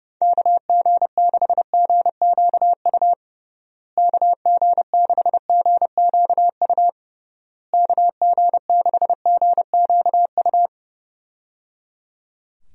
Morse code, heard as a keyed tone, about 31 wpm, 700 Hz. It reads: KG6GQU KG6GQU KG6GQU
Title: USA Call Signs - 6 Characters - Repeated 3x - 30wpm